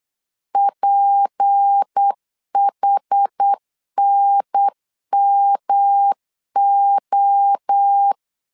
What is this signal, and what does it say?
Morse code, heard as a keyed tone, about 8 wpm, 800 Hz. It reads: PHNMO
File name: LEZIONI / 6 PKX0 / 6 P 8wpm 3025